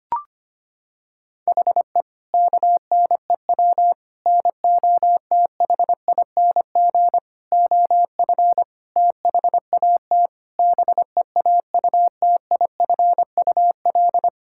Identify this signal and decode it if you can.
Morse code, heard as a keyed tone, about 25 wpm, 700 Hz. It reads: HE KNEW NOTHING OF THAT BEAUTIFUL
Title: Sentences from Top 1600 Words 25wpm